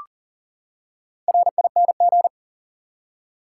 Morse code, heard as a keyed tone, about 40 wpm, 700 Hz. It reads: RING
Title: Four Letter Words - 40wpm